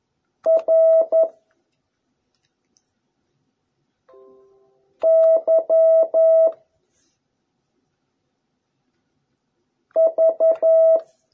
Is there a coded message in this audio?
Yes: Morse code from 0.5 to 11.0 s, heard as a keyed tone, about 11 wpm, 660 Hz. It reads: R Y V